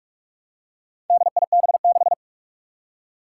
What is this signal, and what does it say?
Morse code, heard as a keyed tone, about 46 wpm, 700 Hz. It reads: DIB6
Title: Four Challenging Letter-Number Combinations 45wpm